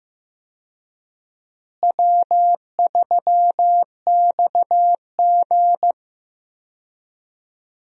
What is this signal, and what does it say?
Morse code, heard as a keyed tone, about 15 wpm, 700 Hz. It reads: W3XG
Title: USA Call Signs - 4 Characters - Repeated 3x - 15wpm